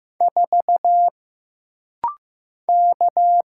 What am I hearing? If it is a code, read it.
Morse code, heard as a keyed tone, about 15 wpm, 700 Hz. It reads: 4 K
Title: Single Letter-Number Alt - Rapid-Fire 15wpm